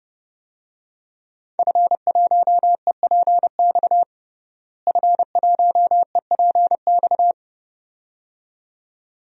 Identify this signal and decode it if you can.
Morse code, heard as a keyed tone, about 30 wpm, 700 Hz. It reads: F1EPX F1EPX
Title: DXCC-100 Call Signs - 5 Characters - Repeated 2x - 30wpm